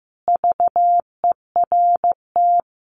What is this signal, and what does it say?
Morse code, heard as a keyed tone, about 15 wpm, 700 Hz. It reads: VERT
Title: Common QSO Phrases 15wpm